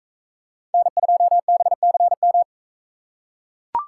Morse code, heard as a keyed tone, about 42 wpm, 700 Hz. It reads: N2BCM